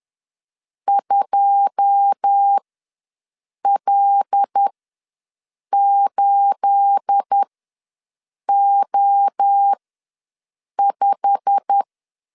Morse code, heard as a keyed tone, about 10 wpm, spaced as for 6 wpm, 800 Hz. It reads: 2L8O5
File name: LEZIONI / 10 SIGNS / 10B INT ws 10wpm 4134